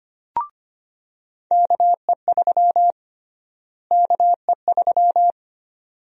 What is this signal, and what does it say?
Morse code, heard as a keyed tone, about 25 wpm, 700 Hz. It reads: KE3 KE3
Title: USA Call Sign Prefixes - Repeated 2x - 25wpm